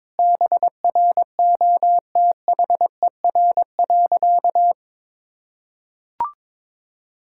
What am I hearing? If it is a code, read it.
Morse code, heard as a keyed tone, about 22 wpm, 700 Hz. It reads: BROTHER.
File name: Difficult Sentences with Hints - 22wpm